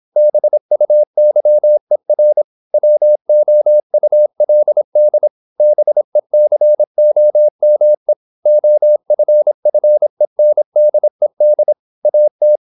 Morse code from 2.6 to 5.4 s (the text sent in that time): WOULD